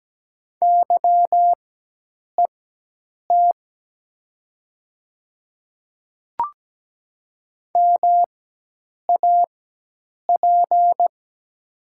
Morse code, heard as a keyed tone, about 17 wpm, spaced as for 8 wpm, 700 Hz. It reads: YET MAP
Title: 4x Character Spacing - Top 500 Words - 17wpm